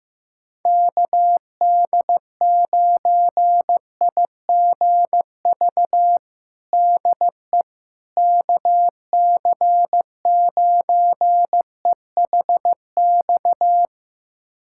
Morse code, heard as a keyed tone, about 15 wpm, 700 Hz. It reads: KD9IGV DE KC9EHX
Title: USA Callsign DE Callsign - 6 Characters - 15wpm